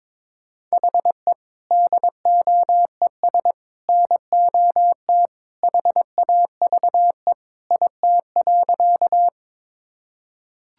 Morse code, heard as a keyed tone, about 22 wpm, 700 Hz. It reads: HE DOES NOT HAVE IT.
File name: practice / 22wpm / Sentences with 3-Word Phrases - 22wpm